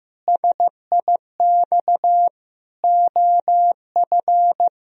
Morse code, heard as a keyed tone, about 15 wpm, 700 Hz. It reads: SIX OF
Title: Sentences from Top 1000 Words - Review Part 2 - 15wpm